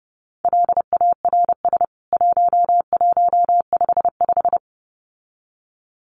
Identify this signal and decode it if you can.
Morse code, heard as a keyed tone, about 30 wpm, 700 Hz. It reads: LARS 1155
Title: CWT Exchange 30wpm